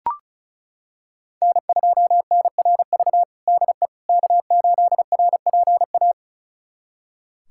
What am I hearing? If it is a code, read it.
Morse code, heard as a keyed tone, about 34 wpm, 700 Hz. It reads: N2NRV DE K8RPA
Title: USA Callsign DE Callsign - 5 Characters - 35wpm